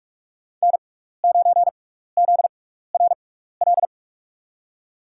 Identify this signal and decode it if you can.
Morse code, heard as a keyed tone, about 44 wpm, spaced as for 15 wpm, 700 Hz. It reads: N9ZRL
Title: USA Call Signs - 6x Character Spacing - 45wpm